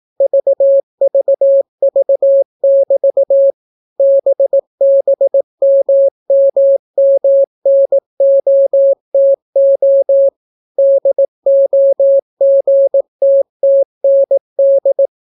Morse code, heard as a keyed tone, about 18 wpm, 550 Hz. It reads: VVV<BT> BBMMMNOTO DOGTTND